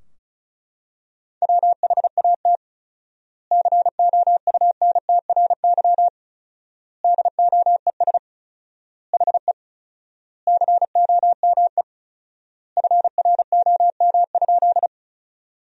Morse code, heard as a keyed tone, about 34 wpm, 700 Hz. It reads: WHAT COUNTRY DOES HE COME FROM?
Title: 4x Word-Spacing - Sentences from Top 300 Words - 35wpm